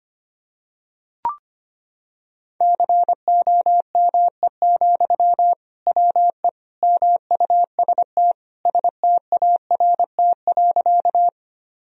Morse code, heard as a keyed tone, about 25 wpm, 700 Hz. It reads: COME, WE MUST START.
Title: Sentences from Top 1000 Words - Review Part 1 - 25wpm